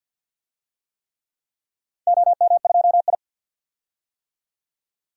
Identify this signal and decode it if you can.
Morse code, heard as a keyed tone, about 46 wpm, 700 Hz. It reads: OM2I
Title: DXCC-100 Call Signs - 4 Characters - 50wpm